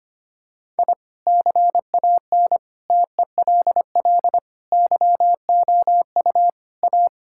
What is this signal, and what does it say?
Morse code, heard as a keyed tone, about 25 wpm, 700 Hz. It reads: I CAN TELL YOU A
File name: practice / 25wpm / Sentences with 4-Word Phrases - 25wpm